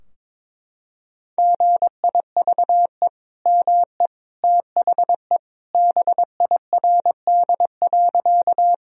Morse code, heard as a keyed tone, about 22 wpm, 700 Hz. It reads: GIVE ME THE BIRD.
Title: Sentences from Top 1000 Words - Review Part 2 - 22wpm